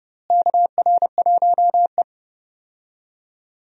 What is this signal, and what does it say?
Morse code, heard as a keyed tone, about 30 wpm, 700 Hz. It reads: KR1E
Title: USA Call Signs - 4 Characters - Repeated 2x - 30wpm